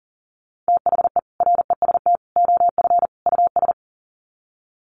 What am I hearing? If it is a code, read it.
Morse code, heard as a keyed tone, about 37 wpm, 700 Hz. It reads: THE REST OF US
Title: Sets of 4 Words 40wpm